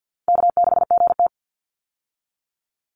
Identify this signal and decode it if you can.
Morse code, heard as a keyed tone, about 50 wpm, 700 Hz. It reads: K6GT
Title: Easy Call Signs Repeated 3x - 50wpm